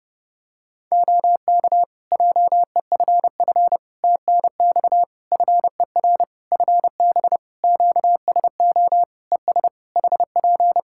Morse code, heard as a keyed tone, about 30 wpm, 700 Hz. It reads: OK JEFF TNX FER FB QSO ES HP
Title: K1USN SST QSO Practice 30wpm